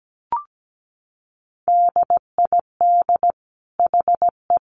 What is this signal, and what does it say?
Morse code, heard as a keyed tone, about 17 wpm, 700 Hz. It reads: DID HE
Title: Sentences from Top 400 Words 17wpm